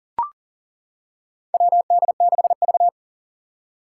Morse code, heard as a keyed tone, about 40 wpm, 700 Hz. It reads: WD6V
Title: Four Challenging Letter-Number Combinations 40wpm